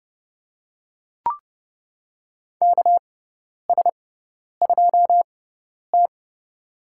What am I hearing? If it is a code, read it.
Morse code, heard as a keyed tone, about 30 wpm, spaced as for 10 wpm, 700 Hz. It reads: KS2T
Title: USA Call Signs - 6x Character Spacing - 30wpm